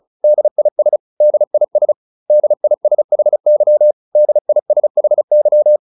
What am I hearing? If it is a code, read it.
Morse code, heard as a keyed tone, about 35 wpm, 600 Hz. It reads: DIS DIS DISHY DISHY